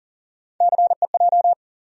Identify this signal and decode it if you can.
Morse code, heard as a keyed tone, about 40 wpm, 700 Hz. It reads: CEJ
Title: USA Call Sign Postfixes - 3 Letters 40wpm